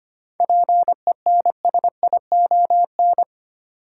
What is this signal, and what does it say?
Morse code, heard as a keyed tone, about 25 wpm, 700 Hz. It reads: PENSION
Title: Seven Letter Words - 25wpm